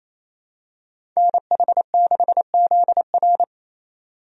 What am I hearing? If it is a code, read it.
Morse code, heard as a keyed tone, about 27 wpm, 700 Hz. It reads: NH6ZR
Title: USA Call Signs - 5 Characters - 28wpm